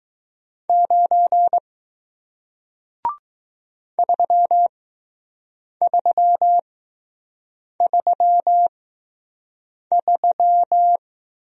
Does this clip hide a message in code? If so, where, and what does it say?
Yes: Morse code from 0.7 to 11.0 s, heard as a keyed tone, about 20 wpm, spaced as for 6 wpm, 700 Hz. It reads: 9 3333